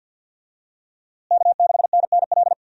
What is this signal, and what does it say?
Morse code, heard as a keyed tone, about 47 wpm, 700 Hz. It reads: K6NNL
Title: USA Call Signs - 5 Characters - Repeated 3x - 50wpm